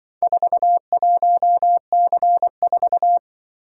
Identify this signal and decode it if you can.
Morse code, heard as a keyed tone, about 25 wpm, 700 Hz. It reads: 41C4
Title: Speed-Racing - Four Letter-Number Combinations - 20wpm